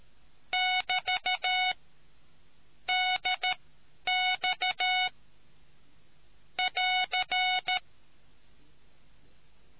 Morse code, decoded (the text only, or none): <BT> DX <AR>